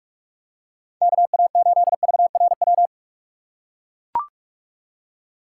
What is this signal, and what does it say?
Morse code, heard as a keyed tone, about 41 wpm, 700 Hz. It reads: KA8VRW
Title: USA Call Signs - 6x Character Spacing - 45wpm